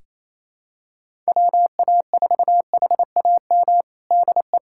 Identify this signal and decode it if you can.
Morse code, heard as a keyed tone, about 28 wpm, 700 Hz. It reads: WA4HAM DE
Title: USA Callsign DE Callsign - 6 Characters - 28wpm